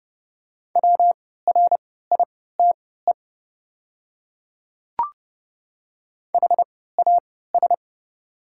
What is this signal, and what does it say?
Morse code, heard as a keyed tone, about 30 wpm, spaced as for 17 wpm, 700 Hz. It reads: WRITE HAS
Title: 3x Character Spacing - Top 100 Words - 30wpm